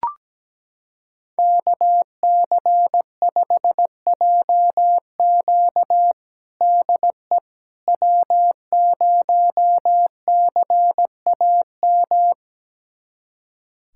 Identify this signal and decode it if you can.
Morse code, heard as a keyed tone, about 17 wpm, 700 Hz. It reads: KC5JQ DE W0CAM
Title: USA Callsign DE Callsign - 5 Characters - 17wpm